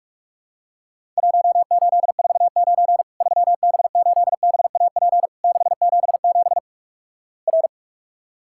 Morse code, heard as a keyed tone, about 48 wpm, 700 Hz. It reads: 1849 3B8BAP 677 R
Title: Worked All Europe DX Contest - QTC Exchange - 45wpm